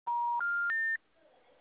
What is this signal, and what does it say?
Special information tone.